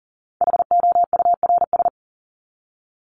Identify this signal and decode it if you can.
Morse code, heard as a keyed tone, about 40 wpm, 700 Hz. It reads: HOURS
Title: ES:S Suffix - Top 100 Words with Base Words 40wpm